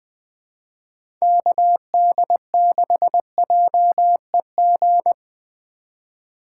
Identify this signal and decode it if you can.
Morse code, heard as a keyed tone, about 20 wpm, 700 Hz. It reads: KD6JEG
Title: USA Call Signs - 2x Character Spacing - 20wpm